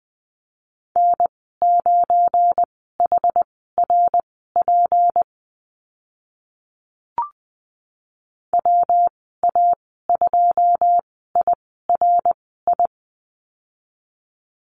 Morse code, heard as a keyed tone, about 20 wpm, spaced as for 15 wpm, 700 Hz. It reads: N9HRP WA2IRI
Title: 186 Lesson 30g - Letter b - Call Signs 20wpm